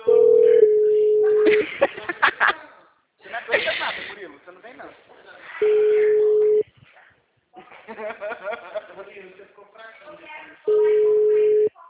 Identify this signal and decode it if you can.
Ringback tone (Europe, 425 Hz), with speech.
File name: out- -2018- - - .267